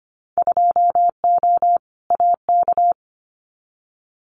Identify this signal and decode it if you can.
Morse code, heard as a keyed tone, about 25 wpm, 700 Hz. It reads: 2O AK
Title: Winter Field Day Exchanges Encore 25wpm